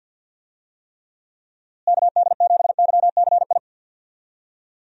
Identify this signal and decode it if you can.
Morse code, heard as a keyed tone, about 52 wpm, 700 Hz. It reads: KD7YCI
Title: USA Call Signs - 6 Characters - Repeated 2x - 50wpm